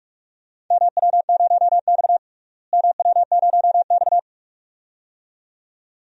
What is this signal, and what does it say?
Morse code, heard as a keyed tone, about 43 wpm, 700 Hz. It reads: MW0X MW0X
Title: DXCC-100 Call Signs - 4 Characters - Repeated 2x - 45wpm